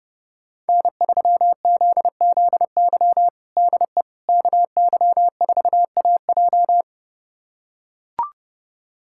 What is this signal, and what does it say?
Morse code, heard as a keyed tone, about 32 wpm, 700 Hz. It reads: N3ZZY DE KY4AJ
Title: USA Callsign DE Callsign - 5 Characters - 30wpm